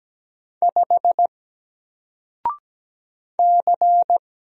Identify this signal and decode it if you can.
Morse code, heard as a keyed tone, about 17 wpm, 700 Hz. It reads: 5 C
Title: Single Letter-Number Alt 17wpm